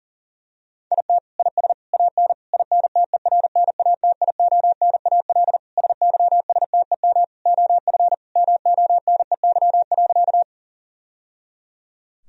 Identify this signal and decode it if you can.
Morse code, heard as a keyed tone, about 37 wpm, 700 Hz. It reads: IT IS AN INTERNATIONAL SYSTEM OF MONEY.
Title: Sentences from Top 1000 Words - Review Part 4 - 40wpm